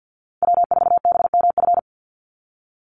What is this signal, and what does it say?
Morse code, heard as a keyed tone, about 48 wpm, 700 Hz. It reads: W4BMF
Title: USA Call Signs 50wpm